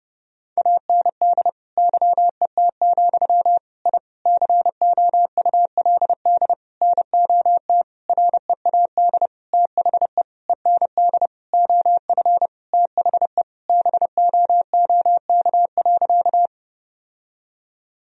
Morse code, heard as a keyed tone, about 30 wpm, 700 Hz. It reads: AND YET, I COULD NOT READ THE END OF THE BOOK.
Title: Sentences from Top 1000 Words - Review Part 1 - 30wpm